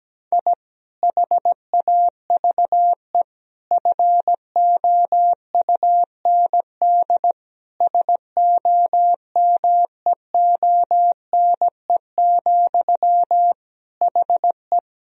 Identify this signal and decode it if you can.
Morse code, heard as a keyed tone, about 17 wpm, 700 Hz. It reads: I HAVE FOUND SOMEONE, HE
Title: Sentences from Top 1000 Words - Review Part 1 - 17wpm